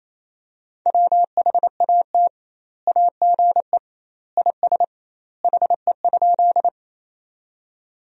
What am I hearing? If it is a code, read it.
Morse code, heard as a keyed tone, about 28 wpm, 700 Hz. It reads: WHAT AGE IS HE?
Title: 2x Word-Spacing - Sentences from Top 500 Words - 28wpm